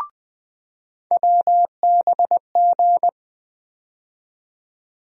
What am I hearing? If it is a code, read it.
Morse code, heard as a keyed tone, about 20 wpm, 700 Hz. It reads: WBG
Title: USA Call Sign Postfixes - 3 Letters 20wpm